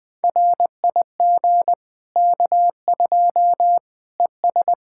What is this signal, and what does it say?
Morse code, heard as a keyed tone, about 20 wpm, 700 Hz. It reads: RIG K2 ES
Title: Practice QSO - Exchange 2 - Info - 20wpm